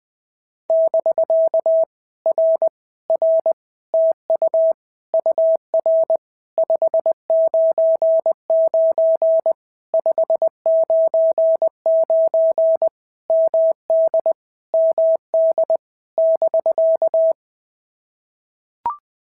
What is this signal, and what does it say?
Morse code, heard as a keyed tone, about 20 wpm, 650 Hz. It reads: <BK> R R TU UR 599 599 MD MD <BK>
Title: POTA QSO Practice - 20wpm